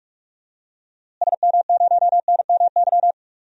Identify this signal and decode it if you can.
Morse code, heard as a keyed tone, about 46 wpm, 700 Hz. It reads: SM0NMY
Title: DXCC-100 Call Signs - 6 Characters - Repeated 2x - 45wpm